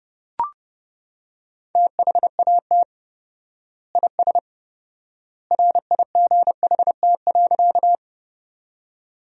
Morse code, heard as a keyed tone, about 32 wpm, 700 Hz. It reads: THAT IS RIGHT.
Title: 4x Word-Spacing - Sentences from Top 200 Words - 30wpm